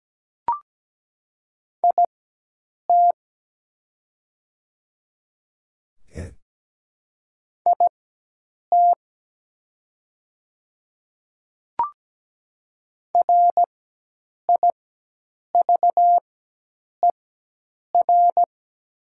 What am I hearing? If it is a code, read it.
Morse code, heard as a keyed tone, about 17 wpm, spaced as for 8 wpm, 700 Hz. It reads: IT IT RIVER
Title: 4x Character Spacing - Top 300 Words - 17wpm